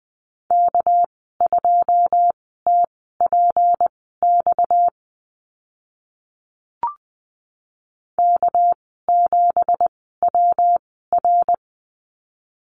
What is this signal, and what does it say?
Morse code, heard as a keyed tone, about 20 wpm, spaced as for 14 wpm, 700 Hz. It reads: K2TPX K7WR
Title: 231 Lesson 37g - Letter x - Call Signs 20wpm